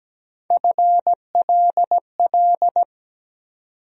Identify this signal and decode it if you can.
Morse code, heard as a keyed tone, about 17 wpm, 700 Hz. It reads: FLL
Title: USA Call Sign Postfixes - 3 Letters 17wpm